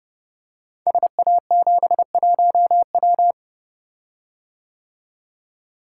Morse code, heard as a keyed tone, about 30 wpm, 700 Hz. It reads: SA71W
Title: Code Groups - 5 Letter:Numbers - 30wpm